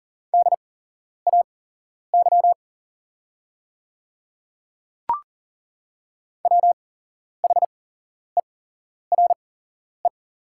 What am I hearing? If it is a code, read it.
Morse code, heard as a keyed tone, about 38 wpm, spaced as for 11 wpm, 700 Hz. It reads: DAY WHERE
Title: 8x Character Spacing - Top 200 Words - 40wpm